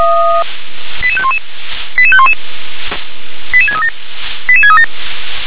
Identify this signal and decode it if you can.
Five-tone selective call (ZVEI), heard as quick runs of five tones, tones of about 70 ms. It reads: 88511 88511 88637 88637